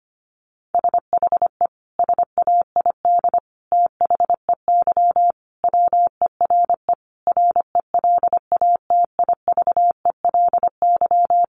Morse code, heard as a keyed tone, about 25 wpm, 700 Hz. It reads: SHE SAID THEY WERE RELATIVELY
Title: Sentences from Top 1100 Words 25wpm